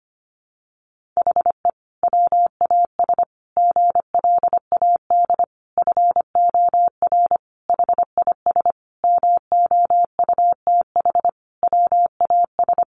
Morse code, heard as a keyed tone, about 25 wpm, 700 Hz. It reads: HE WAS GLAD FOR HIS MOUTH WAS
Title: Sentences from Top 1500 Words 25wpm